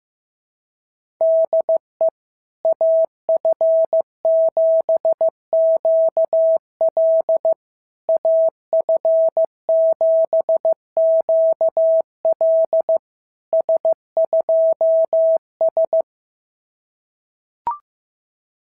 Morse code, heard as a keyed tone, about 16 wpm, 650 Hz. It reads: DE AF7QL AF7QL S2S
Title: SOTA QSO Practice 15wpm